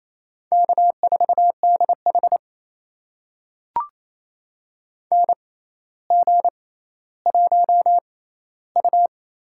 Morse code, heard as a keyed tone, about 28 wpm, spaced as for 14 wpm, 700 Hz. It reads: K4DH NG1U
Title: USA Call Signs - 4 Characters - 6x Character Spacing - 28wpm